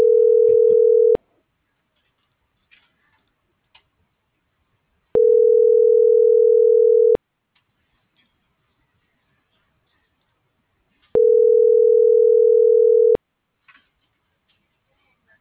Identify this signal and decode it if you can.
Ringback tone (North America), 440 + 480 Hz.